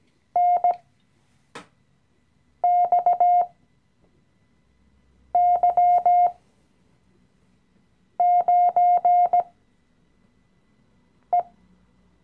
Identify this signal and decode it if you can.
Morse code, heard as a keyed tone, about 17 wpm, spaced as for 4 wpm, 710 Hz. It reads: NXY9E
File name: CW-Online-Kurs / Anfänger / Lektion 016